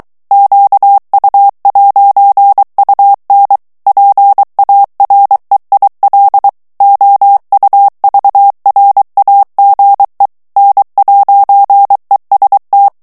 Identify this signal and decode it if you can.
Morse code, heard as a keyed tone, about 23 wpm, 800 Hz. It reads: QU'UN PAREIL OUVRAGE N'EST
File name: FRANCAIS TEXTE CLAIR 2 VITESSE 22 MOTS MINUTES